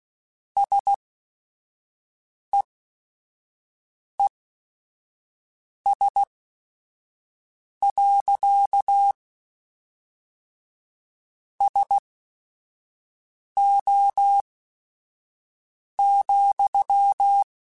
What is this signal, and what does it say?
Morse code, heard as a keyed tone, about 16 wpm, spaced as for 5 wpm, 800 Hz. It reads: SEES. SO,
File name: QSOs / QSOs-mp3s / WD0FBT 1@5